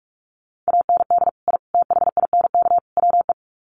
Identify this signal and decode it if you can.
Morse code, heard as a keyed tone, about 44 wpm, 700 Hz. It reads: AND I THINK WE
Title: Sets of 4 Words 45wpm